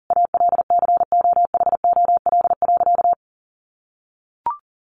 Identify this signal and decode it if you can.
Morse code, heard as a keyed tone, about 40 wpm, 700 Hz. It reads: ALCOHOL.
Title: Sentences from Top 1400 Words 40wpm